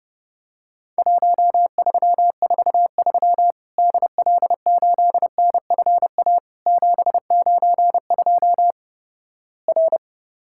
Morse code, heard as a keyed tone, about 30 wpm, 700 Hz. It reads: 1343 DL8NFA 792 R
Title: Worked All Europe DX Contest - QTC Exchange - 30wpm